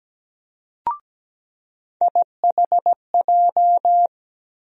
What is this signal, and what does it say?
Morse code, heard as a keyed tone, about 17 wpm, 700 Hz. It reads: IHJ
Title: USA Call Sign Postfixes - 3 Letters 17wpm